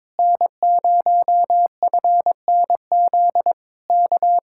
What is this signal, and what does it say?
Morse code, heard as a keyed tone, about 22 wpm, 700 Hz. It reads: N0FNZ K
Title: Speed-Racing Common QSO Phrases Encore - 22wpm